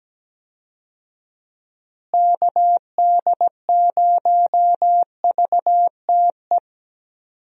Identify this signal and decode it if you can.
Morse code, heard as a keyed tone, about 17 wpm, 700 Hz. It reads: KD0VTE
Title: USA Call Signs - 6 Characters - Repeated 3x - 17wpm